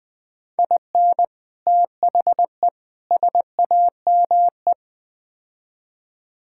Morse code, heard as a keyed tone, about 20 wpm, 700 Hz. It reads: IN THE SAME
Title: Build A Phrase - 4-Word Phrases - 20wpm